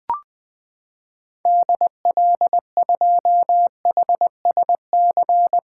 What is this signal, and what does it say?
Morse code, heard as a keyed tone, about 21 wpm, 700 Hz. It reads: DL2HSC
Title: 2023 SOTA Activator Call Signs - 20wpm